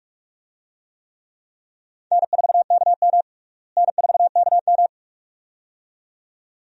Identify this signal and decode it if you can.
Morse code, heard as a keyed tone, about 46 wpm, 700 Hz. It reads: N4KM N4KM
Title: USA Call Signs - 4 Characters - Repeated 2x - 45wpm